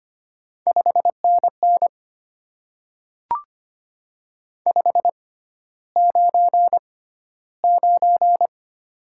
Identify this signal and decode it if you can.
Morse code, heard as a keyed tone, about 25 wpm, spaced as for 13 wpm, 700 Hz. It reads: 5NN 599
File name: Top 90 QSO Elements - 6x Character Spacing - 25wpm